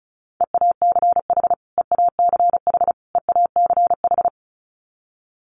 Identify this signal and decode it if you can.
Morse code, heard as a keyed tone, about 34 wpm, 700 Hz. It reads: EACH EACH EACH
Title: Top 500 Words Repeated 35wpm